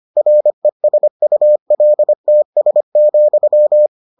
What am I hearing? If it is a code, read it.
Morse code, heard as a keyed tone, about 25 wpm, 600 Hz. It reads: RESULTS,